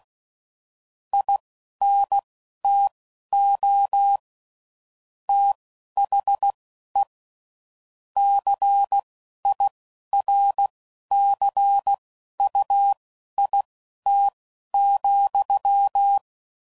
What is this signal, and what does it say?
Morse code, heard as a keyed tone, about 16 wpm, spaced as for 11 wpm, 800 Hz. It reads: INTO THE CIRCUIT,
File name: CW Code Groups @10WPM / Technical Subjects / XC